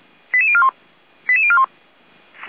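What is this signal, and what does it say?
Five-tone selective call (ZVEI), heard as quick runs of five tones, tones of about 70 ms. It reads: 88041 88041
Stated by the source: ZVEI five-tone call 88650 received by SDR.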